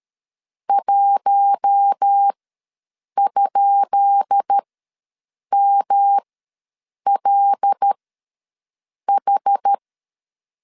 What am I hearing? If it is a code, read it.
Morse code, heard as a keyed tone, about 13 wpm, spaced as for 7 wpm, 800 Hz. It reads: 1?MLH
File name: LEZIONI / 10 SIGNS / 10B INT ws 12wpm 3527